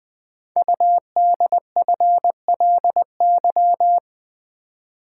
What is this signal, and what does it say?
Morse code, heard as a keyed tone, about 20 wpm, 700 Hz. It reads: UDFLY
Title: Code Groups - 5 Letters - 20wpm